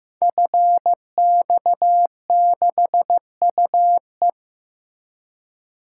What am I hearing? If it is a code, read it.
Morse code, heard as a keyed tone, about 15 wpm, 700 Hz. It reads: FX6UE
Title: Code Groups - 5 Letter:Numbers - 15wpm